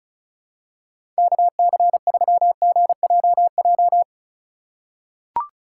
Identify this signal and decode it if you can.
Morse code, heard as a keyed tone, about 34 wpm, 700 Hz. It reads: KC3GJJ
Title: USA Call Signs - 6 Characters - 3x Character Spacing - 35wpm